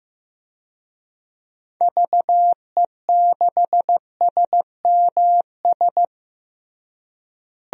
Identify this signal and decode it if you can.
Morse code, heard as a keyed tone, about 16 wpm, 700 Hz. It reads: VE6SMS
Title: DXCC-100 Call Signs - 6 Characters - Repeated 2x - 15wpm